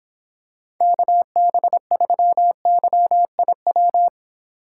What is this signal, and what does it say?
Morse code, heard as a keyed tone, about 26 wpm, 700 Hz. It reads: KB3YIW